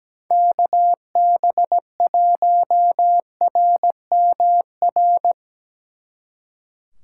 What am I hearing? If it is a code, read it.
Morse code, heard as a keyed tone, about 18 wpm, 700 Hz. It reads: KB1RMR
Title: Speed Racing - USA 2x3 Call Signs 17wpm